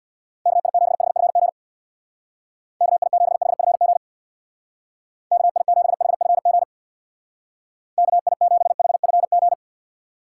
Morse code, heard as a keyed tone, about 61 wpm, 700 Hz. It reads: KI7HFG KI7HFG KI7HFG KI7HFG
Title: Speed Racing - USA Call Signs 50wpm